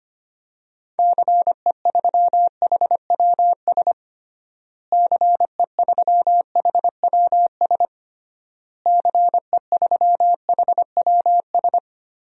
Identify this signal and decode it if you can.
Morse code, heard as a keyed tone, about 25 wpm, 700 Hz. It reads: CE3HWS CE3HWS CE3HWS
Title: DXCC-100 Call Signs - 6 Characters - Repeated 3x - 25wpm